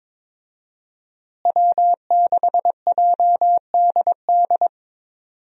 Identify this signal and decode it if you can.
Morse code, heard as a keyed tone, about 21 wpm, 700 Hz. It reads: W6JDD
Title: USA Call Signs - 5 Characters - Repeated 2x - 22wpm